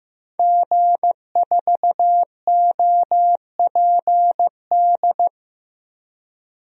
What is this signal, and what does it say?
Morse code, heard as a keyed tone, about 16 wpm, 700 Hz. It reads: G4OPD
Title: DXCC-100 Call Signs - 5 Characters - Repeated 3x - 15wpm